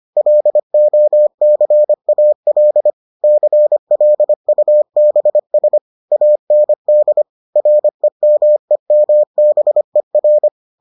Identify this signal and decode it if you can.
Morse code, heard as a keyed tone, about 25 wpm, 600 Hz. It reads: LOCAL CLUBS AND REMEMBER